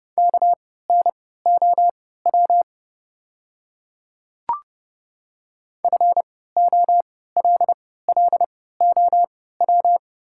Morse code, heard as a keyed tone, about 29 wpm, spaced as for 17 wpm, 690 Hz. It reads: KNOW FOLLOW
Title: 3x Character Spacing - Top 300 Words - 30wpm